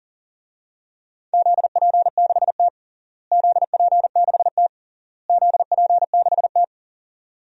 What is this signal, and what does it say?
Morse code, heard as a keyed tone, about 40 wpm, 700 Hz. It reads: ZP6T ZP6T ZP6T
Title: DXCC-100 Call Signs - 4 Characters - Repeated 3x - 40wpm